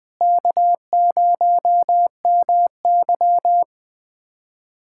Morse code, heard as a keyed tone, about 20 wpm, 700 Hz. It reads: K0MY
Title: USA Callsign DE Callsign - 4 Characters - 20wpm